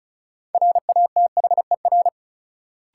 Morse code, heard as a keyed tone, about 36 wpm, 700 Hz. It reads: RATHER
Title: Sentences from Top 700 Words - New Word Drill - 35wpm